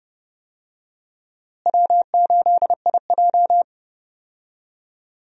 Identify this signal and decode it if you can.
Morse code, heard as a keyed tone, about 32 wpm, 700 Hz. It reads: W8IJ
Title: USA Call Signs - 4 Characters - Repeated 2x - 30wpm